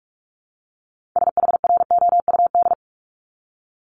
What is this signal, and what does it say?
Morse code, heard as a keyed tone, about 46 wpm, 700 Hz. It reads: SHROUD
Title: Multi-letter Phonemes Repeated 3x 45wpm